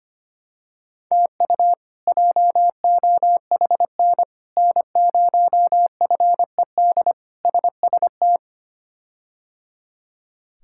Morse code, heard as a keyed tone, about 25 wpm, 700 Hz. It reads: TU JOHN N0FED SST
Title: K1USN SST QSO Practice - 25wpm